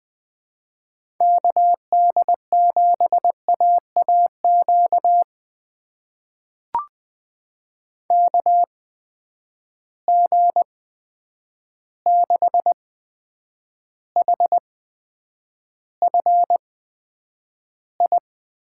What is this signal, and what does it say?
Morse code, heard as a keyed tone, about 20 wpm, spaced as for 9 wpm, 700 Hz. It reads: KD7AAQ KG6HFI
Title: USA Call Signs - 6 Characters - 8x Character Spacing - 20wpm